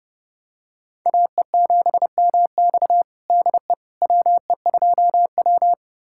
Morse code, heard as a keyed tone, about 32 wpm, 700 Hz. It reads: AE7MX DE WE2W
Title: USA Callsign DE Callsign 30wpm